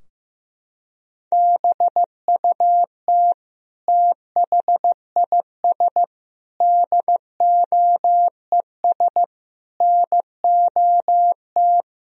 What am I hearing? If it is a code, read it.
Morse code, heard as a keyed tone, about 16 wpm, 700 Hz. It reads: BUT THIS DOES NOT
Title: Sentences from Top 1000 Words - Review Part 2 - 15wpm